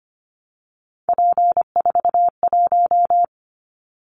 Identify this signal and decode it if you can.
Morse code, heard as a keyed tone, about 25 wpm, 700 Hz. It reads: P41
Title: Callsign Prefixes for Top 100 DXCC - 25wpm